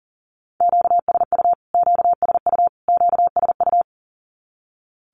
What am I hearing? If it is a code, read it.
Morse code, heard as a keyed tone, about 40 wpm, 700 Hz. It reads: QSU QSU QSU
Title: Q-Codes Repeated 3x 40wpm